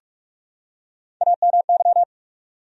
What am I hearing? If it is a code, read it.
Morse code, heard as a keyed tone, about 44 wpm, 700 Hz. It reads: AMY
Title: USA Common Names - Max 4 Letters - 45wpm